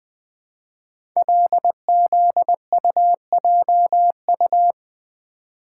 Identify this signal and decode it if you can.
Morse code, heard as a keyed tone, about 20 wpm, 700 Hz. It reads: LZUJU
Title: Code Groups - 5 Letters - 20wpm